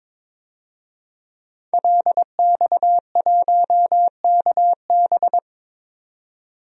Morse code, heard as a keyed tone, about 21 wpm, 700 Hz. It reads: LX1KB
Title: DXCC-100 Call Signs - 5 Characters - 22wpm